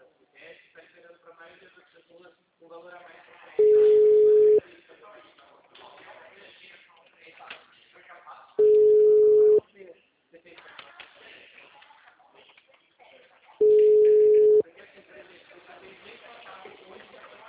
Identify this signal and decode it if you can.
Ringback tone (Europe, 425 Hz), with speech.